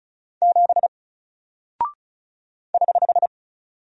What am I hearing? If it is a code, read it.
Morse code, heard as a keyed tone, about 36 wpm, 700 Hz. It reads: 7 <HH>
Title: Single Character (Letter:Number:Punctuation:Prosign) Alt 35wpm